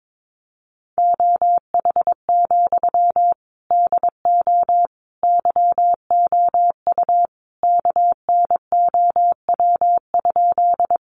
Morse code, heard as a keyed tone, about 22 wpm, 700 Hz. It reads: OH, DO YOU KNOW?
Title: Sentences from Top 500 Words 22wpm